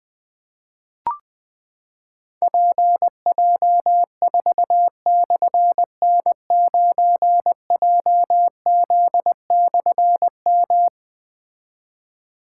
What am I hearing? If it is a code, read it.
Morse code, heard as a keyed tone, about 20 wpm, 700 Hz. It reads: PJ4/N9JZ/M
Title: Difficult Call Signs 20wpm